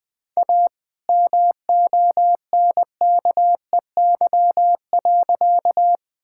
Morse code, heard as a keyed tone, about 20 wpm, 700 Hz. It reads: A MONKEY.
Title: All-Age Appropriate Jokes - 20wpm